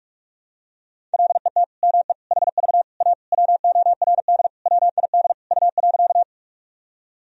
Morse code, heard as a keyed tone, about 41 wpm, 700 Hz. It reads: LET ME HV A WORD WID U.
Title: Sentences from Top 300 Words and Abbreviations 45wpm